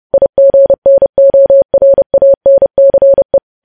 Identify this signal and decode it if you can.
Morse code, heard as a keyed tone, about 30 wpm, 550 Hz. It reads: IGNORANCE
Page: news headlines in morse code at 30 wpm